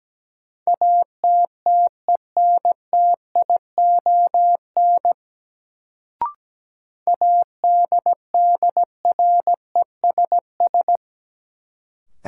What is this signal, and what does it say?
Morse code, heard as a keyed tone, about 17 wpm, 700 Hz. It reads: ATTENTION ADDRESS
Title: Sentences from Top 1000 Words - New Word Drill - 17wpm